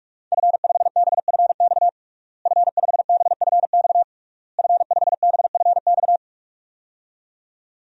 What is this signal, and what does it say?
Morse code, heard as a keyed tone, about 41 wpm, 700 Hz. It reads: F5BFX F5BFX F5BFX
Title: DXCC-100 Call Signs - 5 Characters - Repeated 3x - 45wpm